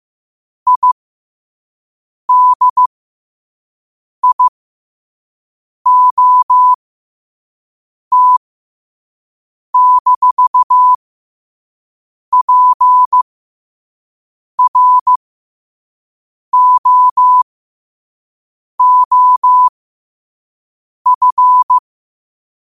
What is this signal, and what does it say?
Morse code, heard as a keyed tone, about 15 wpm, spaced as for 5 wpm, 1000 Hz. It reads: IDIOT-PROOF
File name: qotdfiles / 5wpm